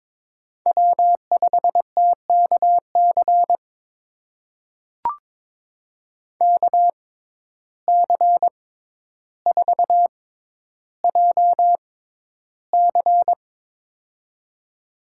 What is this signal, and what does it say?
Morse code, heard as a keyed tone, about 22 wpm, spaced as for 11 wpm, 700 Hz. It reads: W5TKC KC4JC
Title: USA Call Signs - 5 Characters - 6x Character Spacing - 22wpm